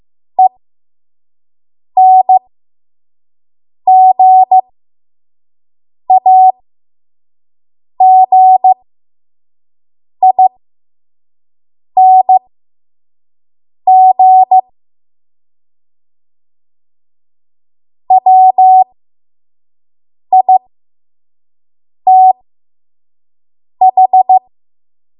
Morse code, heard as a keyed tone, about 15 wpm, spaced as for 5 wpm, 750 Hz. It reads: ENGAGING WITH
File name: Morse / 05WPM